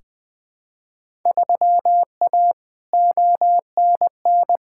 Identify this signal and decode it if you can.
Morse code, heard as a keyed tone, about 20 wpm, 700 Hz. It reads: 3A ONN
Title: Field Day Exchanges Encore - 20wpm